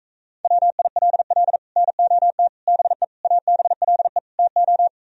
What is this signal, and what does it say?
Morse code, heard as a keyed tone, about 44 wpm, 700 Hz. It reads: WILL NOT BE ABLE TO